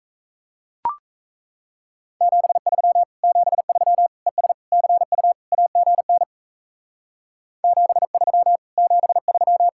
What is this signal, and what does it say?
Morse code, heard as a keyed tone, about 33 wpm, 700 Hz. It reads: 73 73 ES CU AGN 73 73
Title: Speed-Racing Common QSO Phrases - 28wpm